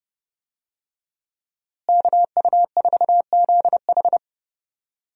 Morse code, heard as a keyed tone, about 32 wpm, 700 Hz. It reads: KU4ZH
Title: USA Call Signs - 5 Characters - Repeated 3x - 30wpm